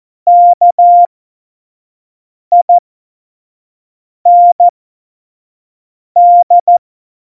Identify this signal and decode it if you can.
Morse code, heard as a keyed tone, about 14 wpm, spaced as for 5 wpm, 700 Hz. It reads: KIND
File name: Morsekurs / Witze / Witz 01. Dezember, 2020 25 F70